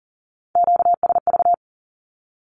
Morse code, heard as a keyed tone, about 42 wpm, 700 Hz. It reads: QSV
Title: Q-Codes with Spoken Meaning - 40wpm